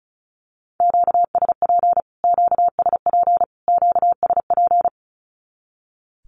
Morse code, heard as a keyed tone, about 34 wpm, 700 Hz. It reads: QSP QSP QSP
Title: Q-Codes Repeated 3x 35wpm